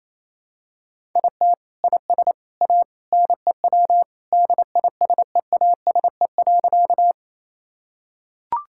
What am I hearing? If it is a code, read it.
Morse code, heard as a keyed tone, about 28 wpm, 700 Hz. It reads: IT IS A NEW DISEASE.
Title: Sentences from Top 1000 Words - Review Part 6 - 28wpm